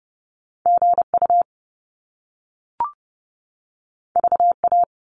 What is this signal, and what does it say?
Morse code, heard as a keyed tone, about 30 wpm, 700 Hz. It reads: GU VA
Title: Top Two Letter Combinations in Words 30wpm